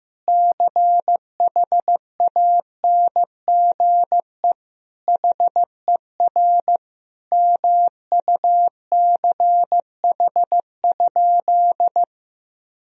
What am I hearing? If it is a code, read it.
Morse code, heard as a keyed tone, about 15 wpm, 700 Hz. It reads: CHANGE HER MUCH?
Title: Sentences from Top 1000 Words - Review Part 1 - 15wpm